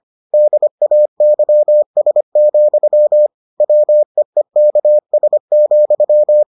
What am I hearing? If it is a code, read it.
Morse code, heard as a keyed tone, about 25 wpm, 600 Hz. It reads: DAYS, WEEKS,